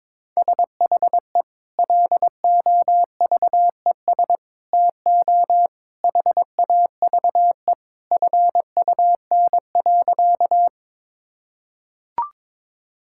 Morse code, heard as a keyed tone, about 22 wpm, 700 Hz. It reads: SHE LOVES TO HAVE FUN.
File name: practice / 22wpm / Sentences with 2-Word Phrases - 22wpm